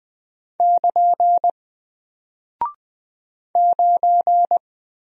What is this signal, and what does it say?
Morse code, heard as a keyed tone, about 20 wpm, 700 Hz. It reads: <KN> 9
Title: Single Character (Letter:Number:Punctuation:Prosign) Alt 20wpm